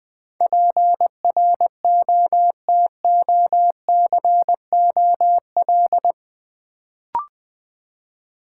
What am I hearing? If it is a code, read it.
Morse code, heard as a keyed tone, about 20 wpm, 700 Hz. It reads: PROTOCOL
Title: Build A Word - 4 Letters At A Time - 8-Letter Words - 20wpm